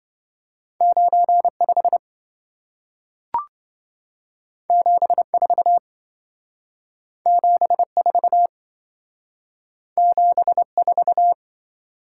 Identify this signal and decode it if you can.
Morse code, heard as a keyed tone, about 27 wpm, 700 Hz. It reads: 95 74 74 74